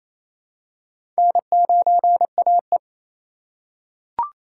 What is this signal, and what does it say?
Morse code, heard as a keyed tone, about 28 wpm, 700 Hz. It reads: N9AE
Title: USA Call Signs - 4 Characters - 2x Character Spacing - 28wpm